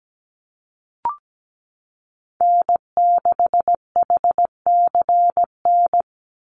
Morse code, heard as a keyed tone, about 17 wpm, 700 Hz. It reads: N6HCN
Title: Call Signs Repeated 2x - 17wpm